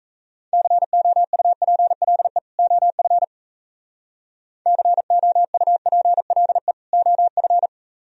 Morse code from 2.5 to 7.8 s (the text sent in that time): OF COUPLE OF